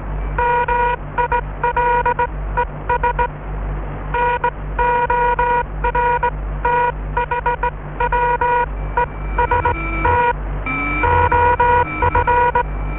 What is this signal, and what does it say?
Morse code, heard as a keyed tone, about 19 wpm, spaced as for 16 wpm, 960 Hz. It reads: MILES NORTHWEST OF